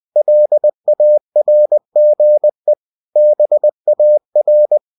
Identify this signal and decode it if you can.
Morse code, heard as a keyed tone, about 20 wpm, 600 Hz. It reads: LARGE BAR
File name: cw-books / winnie-the-pooh / winnie-20wpm-page-0072